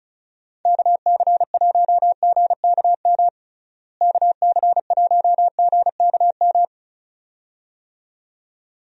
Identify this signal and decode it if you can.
Morse code, heard as a keyed tone, about 36 wpm, 700 Hz. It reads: KC1GKM KC1GKM
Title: USA Call Signs - 6 Characters - Repeated 2x - 35wpm